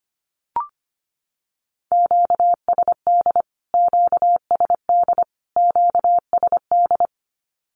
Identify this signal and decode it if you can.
Morse code, heard as a keyed tone, about 25 wpm, 700 Hz. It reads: QSD QSD QSD
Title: Q-Codes Repeated 3x 25wpm